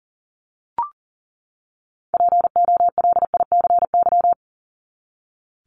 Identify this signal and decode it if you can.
Morse code, heard as a keyed tone, about 40 wpm, 700 Hz. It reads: POLICY
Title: Top 501-750 Words 40wpm